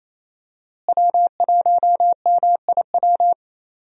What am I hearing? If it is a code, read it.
Morse code, heard as a keyed tone, about 27 wpm, 700 Hz. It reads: W1MIW
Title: USA Call Signs - 5 Characters - Repeated 3x - 28wpm